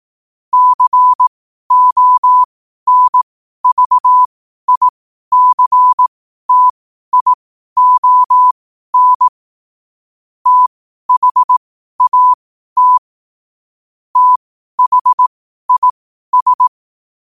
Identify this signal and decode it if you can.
Morse code, heard as a keyed tone, about 18 wpm, spaced as for 12 wpm, 1000 Hz. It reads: CONVICTION THAT THIS
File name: qotdfiles / 12wpm